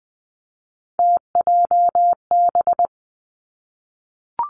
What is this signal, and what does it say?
Morse code, heard as a keyed tone, about 20 wpm, 700 Hz. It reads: TJB
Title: Three Challenging Letter-Number Combinations 20wpm